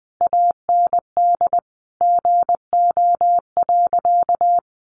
Morse code, heard as a keyed tone, about 20 wpm, 700 Hz. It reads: AND GO.
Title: Sentences from Top 1200 Words 20wpm